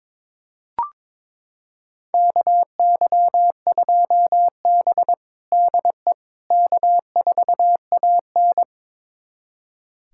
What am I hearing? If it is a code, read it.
Morse code, heard as a keyed tone, about 22 wpm, 700 Hz. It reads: KY2B DE K4AN
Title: USA Callsign DE Callsign - 4 Characters - 22wpm